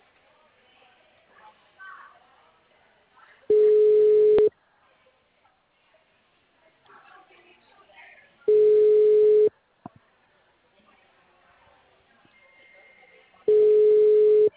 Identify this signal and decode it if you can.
Ringback tone (Europe, 425 Hz), with speech.